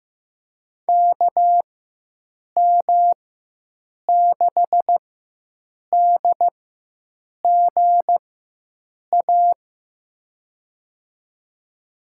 Morse code, heard as a keyed tone, about 16 wpm, spaced as for 7 wpm, 700 Hz. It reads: KM6DGA
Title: USA Call Signs - 6 Characters - 4x Character Spacing - 15wpm